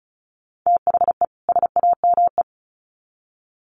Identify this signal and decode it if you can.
Morse code, heard as a keyed tone, about 34 wpm, 700 Hz. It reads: THE SAME
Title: Sets of 2 Words 35wpm